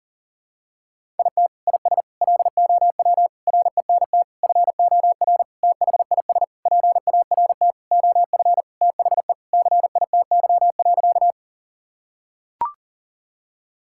Morse code, heard as a keyed tone, about 40 wpm, 700 Hz. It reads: IT IS LOW RENT FOR THIS PART OF THE CITY.
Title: Sentences from Top 1000 Words - Review Part 5 - 40wpm